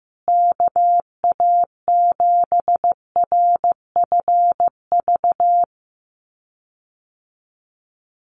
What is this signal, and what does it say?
Morse code, heard as a keyed tone, about 15 wpm, 700 Hz. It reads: KA7RFV
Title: USA Call Signs Repeated 2x 15wpm